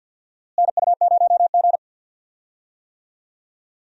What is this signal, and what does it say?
Morse code, heard as a keyed tone, about 53 wpm, 700 Hz. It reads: NU0G
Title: USA Call Signs - 4 Characters - Repeated 2x - 50wpm